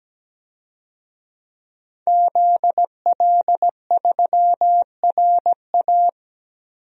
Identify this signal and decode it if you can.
Morse code, heard as a keyed tone, about 17 wpm, 700 Hz. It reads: ZL3RA
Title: DXCC-100 Call Signs - 5 Characters - 17wpm